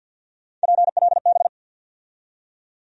Morse code, heard as a keyed tone, about 53 wpm, 700 Hz. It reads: PLB
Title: USA Call Sign Postfixes - 3 Letters 50wpm